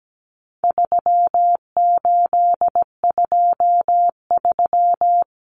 Morse code, heard as a keyed tone, about 17 wpm, 700 Hz. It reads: 3823
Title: Common Cut Numbers - 4-Digit 17wpm